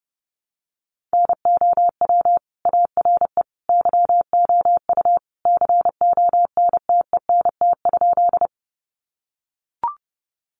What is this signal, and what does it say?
Morse code, heard as a keyed tone, about 30 wpm, 700 Hz. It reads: NOW ARE YOU CONTENT?
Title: Sentences from Top 900 Words 30wpm